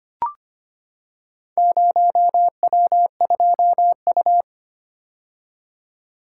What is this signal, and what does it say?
Morse code, heard as a keyed tone, about 25 wpm, 700 Hz. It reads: 0W2U
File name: practice / 25wpm / Four Letter-Number Combinations - 25wpm